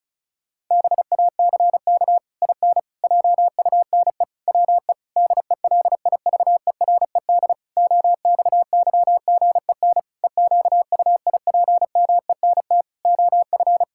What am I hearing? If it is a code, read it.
Morse code, heard as a keyed tone, about 34 wpm, 700 Hz. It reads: BACK IN JUNE WE DELIVERED OXYGEN EQUIPMENT OF
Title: Pangram Challenge v2 - 35wpm